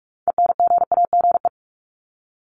Morse code, heard as a keyed tone, about 44 wpm, 700 Hz. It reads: ENGAGE
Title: Top 1251-1500 Words 45wpm